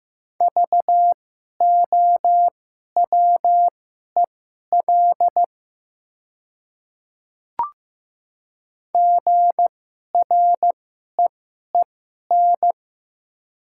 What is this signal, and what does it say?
Morse code, heard as a keyed tone, about 15 wpm, spaced as for 11 wpm, 690 Hz. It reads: VOWEL GREEN
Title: 2x Character Spacing - Top 500 Words - 15wpm